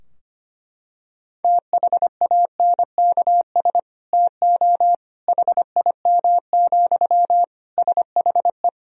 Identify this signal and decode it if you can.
Morse code, heard as a keyed tone, about 25 wpm, 700 Hz. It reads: THANKS TO HIM, SHE
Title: Sentences from Top 1000 Words - Review Part 3 - 25wpm